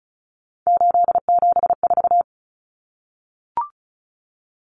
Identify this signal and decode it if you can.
Morse code, heard as a keyed tone, about 34 wpm, 700 Hz. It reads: 874
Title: Numbers - 3-Digit 35wpm